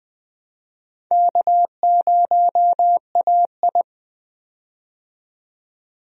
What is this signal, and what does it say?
Morse code, heard as a keyed tone, about 20 wpm, 700 Hz. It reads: K0AI
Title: USA Call Signs - 4 Characters - 20wpm